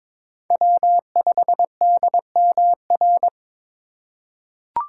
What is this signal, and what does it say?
Morse code, heard as a keyed tone, about 22 wpm, 700 Hz. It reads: W5DMR
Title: USA Call Signs - 5 Characters - 3x Character Spacing - 22wpm